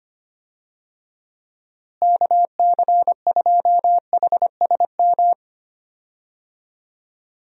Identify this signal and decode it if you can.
Morse code, heard as a keyed tone, about 25 wpm, 700 Hz. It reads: KC2HSM
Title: USA Call Signs - 6 Characters - Repeated 2x - 25wpm